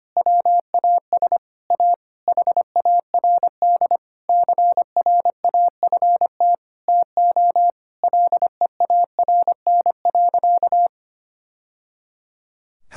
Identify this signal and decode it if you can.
Morse code, heard as a keyed tone, about 25 wpm, 700 Hz. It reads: WAS A HARD CRAFT TO LEARN.
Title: Sentences from Top 1000 Words - Review Part 6 - 25wpm